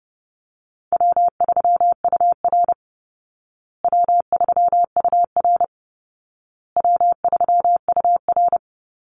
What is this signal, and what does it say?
Morse code, heard as a keyed tone, about 30 wpm, 700 Hz. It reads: W3UR W3UR W3UR
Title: Easy Call Signs Repeated 3x - 30wpm